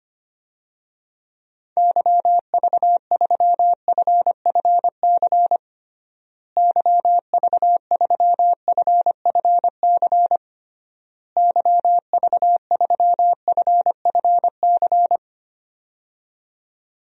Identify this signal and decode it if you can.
Morse code, heard as a keyed tone, about 25 wpm, 700 Hz. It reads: YV3FFC YV3FFC YV3FFC
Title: DXCC-100 Call Signs - 6 Characters - Repeated 3x - 25wpm